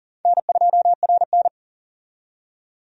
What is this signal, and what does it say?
Morse code, heard as a keyed tone, about 40 wpm, 700 Hz. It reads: N2RN
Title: USA Call Signs - 4 Characters - 40wpm